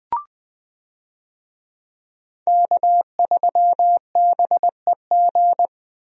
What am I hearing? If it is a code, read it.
Morse code, heard as a keyed tone, about 20 wpm, 700 Hz. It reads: K3BEG
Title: USA Call Signs - 5 Characters - Repeated 2x - 20wpm